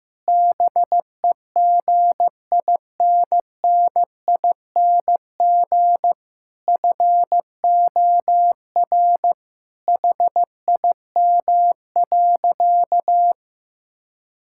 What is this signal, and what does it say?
Morse code, heard as a keyed tone, about 15 wpm, 700 Hz. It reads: BEGINNING FOR HIM.
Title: Sentences from Top 1000 Words - Review Part 5 - 15wpm